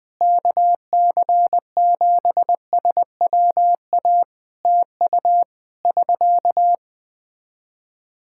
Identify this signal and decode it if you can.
Morse code, heard as a keyed tone, about 21 wpm, 700 Hz. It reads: KC7SWA TU <SK>
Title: Practice QSO - Exchange 4 - Ending - 20wpm